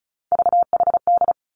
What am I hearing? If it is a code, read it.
Morse code, heard as a keyed tone, about 36 wpm, 700 Hz. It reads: VHD
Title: Three Challenging Letter-Number Combinations 35wpm